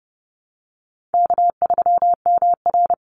Morse code, heard as a keyed tone, about 30 wpm, 700 Hz. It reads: K3MR
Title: Call Signs - 30wpm